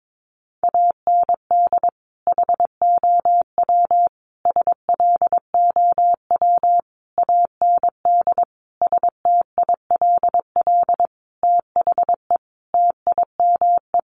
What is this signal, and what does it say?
Morse code, heard as a keyed tone, about 22 wpm, 700 Hz. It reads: AND HOW SLOW AND STILL THE TIME
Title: Sentences from Top 1500 Words 22wpm